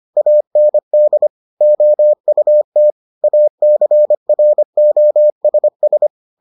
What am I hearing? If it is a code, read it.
Morse code, heard as a keyed tone, about 25 wpm, 600 Hz. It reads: AND OUT ACROSS